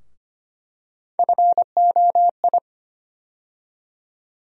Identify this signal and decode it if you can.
Morse code, heard as a keyed tone, about 25 wpm, 700 Hz. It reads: FOI
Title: USA Call Sign Postfixes - 3 Letters 25wpm